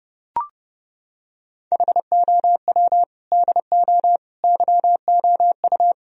Morse code, heard as a keyed tone, about 29 wpm, 700 Hz. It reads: HOW DO YOU